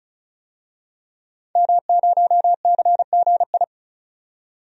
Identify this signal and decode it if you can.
Morse code, heard as a keyed tone, about 36 wpm, 700 Hz. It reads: M0CGI
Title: DXCC-100 Call Signs - 5 Characters - 35wpm